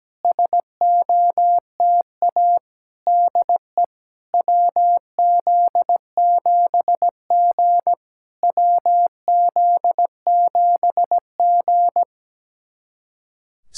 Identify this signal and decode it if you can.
Morse code, heard as a keyed tone, about 17 wpm, 690 Hz. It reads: SOTA DE WZ7G WZ7G
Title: SOTA QSO Practice 17wpm